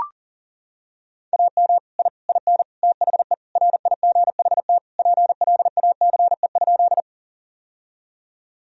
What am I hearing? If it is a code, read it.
Morse code, heard as a keyed tone, about 40 wpm, 700 Hz. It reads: AM I IN THE RIGHT PLACE?
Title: Sentences from Top 1000 Words - Review Part 2 - 40wpm